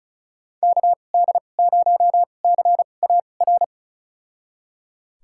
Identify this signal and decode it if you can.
Morse code, heard as a keyed tone, about 36 wpm, spaced as for 25 wpm, 700 Hz. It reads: KD0CAR
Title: USA Call Signs - 6 Characters - 2x Character Spacing - 35wpm